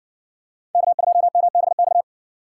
Morse code, heard as a keyed tone, about 60 wpm, 700 Hz. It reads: K2MBX